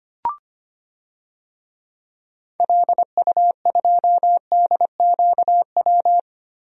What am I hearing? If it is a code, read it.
Morse code, heard as a keyed tone, about 25 wpm, 700 Hz. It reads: LU2DQW
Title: DXCC-100 Call Signs - 6 Characters - Repeated 3x - 25wpm